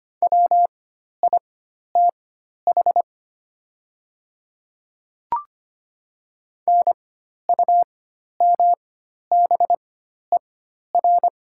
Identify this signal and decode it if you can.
Morse code, heard as a keyed tone, about 25 wpm, spaced as for 12 wpm, 700 Hz. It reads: WITH NUMBER
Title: 4x Character Spacing - Top 100 Words - 25wpm